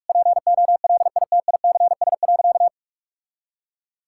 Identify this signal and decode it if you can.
Morse code, heard as a keyed tone, about 46 wpm, 700 Hz. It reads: POLITICS.
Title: Sentences from Top 1000 Words - Review Part 6 - 45wpm